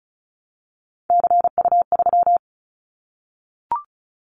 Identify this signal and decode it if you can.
Morse code, heard as a keyed tone, about 36 wpm, 700 Hz. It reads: CU3
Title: Callsign Prefixes for Top 100 DXCC - 35wpm